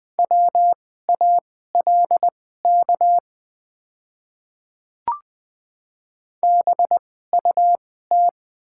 Morse code, heard as a keyed tone, about 20 wpm, spaced as for 14 wpm, 700 Hz. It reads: WALK BUT
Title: 2x Character Spacing - Top 300 Words - 20wpm